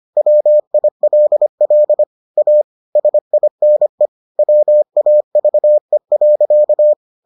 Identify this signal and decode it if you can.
Morse code, heard as a keyed tone, about 25 wpm, 600 Hz. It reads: WILL A SINE WAVE.